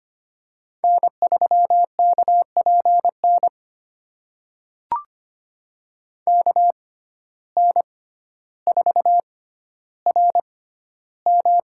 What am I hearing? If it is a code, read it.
Morse code, heard as a keyed tone, about 25 wpm, spaced as for 13 wpm, 700 Hz. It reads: N3KPN KN4RM
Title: USA Call Signs - 5 Characters - 6x Character Spacing - 25wpm